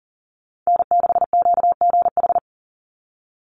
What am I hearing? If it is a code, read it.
Morse code, heard as a keyed tone, about 40 wpm, 700 Hz. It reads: N6QGH
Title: USA Call Signs 40wpm